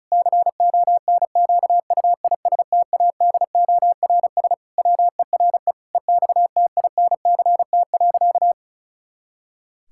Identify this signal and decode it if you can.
Morse code, heard as a keyed tone, about 34 wpm, 700 Hz. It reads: CONQUISTADORS WERE EXTINCT.
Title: Pangram Challenge v2 - 35wpm